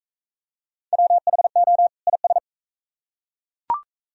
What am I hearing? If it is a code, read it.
Morse code, heard as a keyed tone, about 34 wpm, 690 Hz. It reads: WHO IS
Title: Speed-Racing - Sets of 2 Words 28wpm